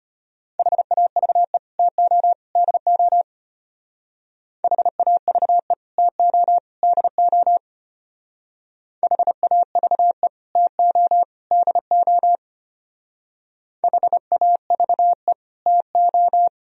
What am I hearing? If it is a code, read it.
Morse code, heard as a keyed tone, about 30 wpm, 700 Hz. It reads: HAVE TO DO HAVE TO DO HAVE TO DO HAVE TO
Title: Speed-Racing - Sets of 3 Words 25wpm